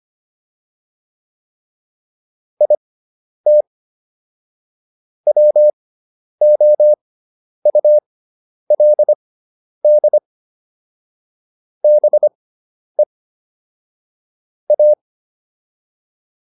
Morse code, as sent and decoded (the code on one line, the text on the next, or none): .. - / .-- --- ..- .-.. -.. / -... / .-
IT WOULD BE A